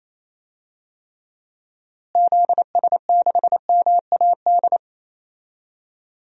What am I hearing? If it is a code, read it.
Morse code, heard as a keyed tone, about 28 wpm, 700 Hz. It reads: ZS6MAD
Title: DXCC-100 Call Signs - 6 Characters - Repeated 2x - 28wpm